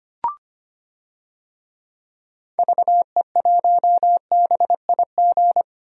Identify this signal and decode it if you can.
Morse code, heard as a keyed tone, about 25 wpm, 700 Hz. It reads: VE1BIG
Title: DXCC-100 Call Signs - 6 Characters - Repeated 3x - 25wpm